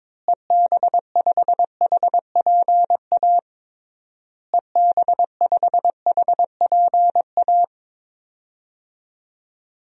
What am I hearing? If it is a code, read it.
Morse code, heard as a keyed tone, about 22 wpm, 700 Hz. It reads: EB5HPA EB5HPA
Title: DXCC-100 Call Signs - 6 Characters - Repeated 3x - 22wpm